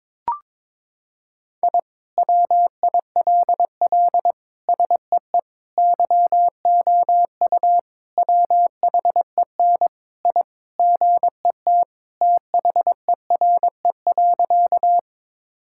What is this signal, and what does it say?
Morse code, heard as a keyed tone, about 22 wpm, 700 Hz. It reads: I WILL SEE YOU WHEN I GET THERE.
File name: practice / 22wpm / Sentences with 2-Word Phrases - 22wpm